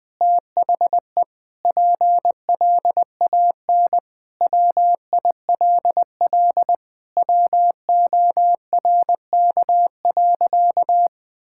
Morse code, heard as a keyed tone, about 20 wpm, 700 Hz. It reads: THE PLAN WILL WORK.
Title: Sentences with 6-Word Phrases - 20wpm